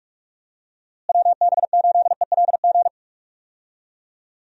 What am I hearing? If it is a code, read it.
Morse code, heard as a keyed tone, about 46 wpm, 700 Hz. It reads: WB8ELG
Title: USA Call Signs - 6 Characters - Repeated 2x - 45wpm